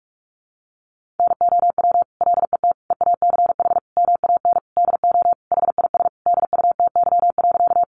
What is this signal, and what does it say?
Morse code, heard as a keyed tone, about 44 wpm, 700 Hz. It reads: NOW LET EACH MAN DO HIS DUTY.
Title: Sentences from Top 1500 Words 45wpm